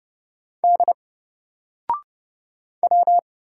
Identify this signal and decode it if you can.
Morse code, heard as a keyed tone, about 30 wpm, 700 Hz. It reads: D W
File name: Single-Letter-Number-30wpm